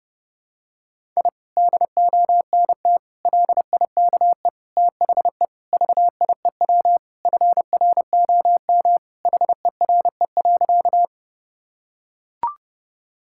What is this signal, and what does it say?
Morse code, heard as a keyed tone, about 30 wpm, 700 Hz. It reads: I DONT LIKE THE VIEW FROM HERE.
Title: Sentences from Top 1000 Words - Review Part 6 - 30wpm